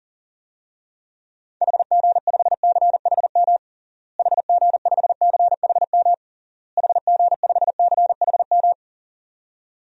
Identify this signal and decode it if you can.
Morse code, heard as a keyed tone, about 37 wpm, 700 Hz. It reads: HG5CHM HG5CHM HG5CHM
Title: DXCC-100 Call Signs - 6 Characters - Repeated 3x - 40wpm